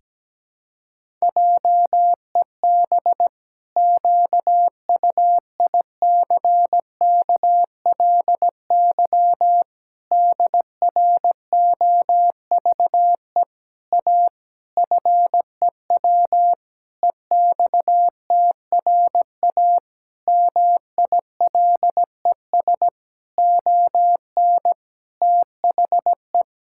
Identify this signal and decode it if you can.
Morse code, heard as a keyed tone, about 17 wpm, 700 Hz. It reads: JEB QUICKLY DROVE A FEW EXTRA MILES ON THE
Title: Pangram Challenge v2 - 17wpm